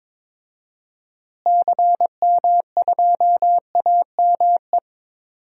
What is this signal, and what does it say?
Morse code, heard as a keyed tone, about 22 wpm, 700 Hz. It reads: CM2AME
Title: DXCC-100 Call Signs - 6 Characters - Repeated 2x - 22wpm